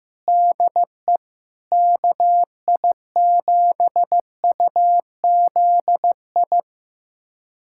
Morse code, heard as a keyed tone, about 15 wpm, 700 Hz. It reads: DE KI7UZI
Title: USA Callsign DE Callsign - 6 Characters - 15wpm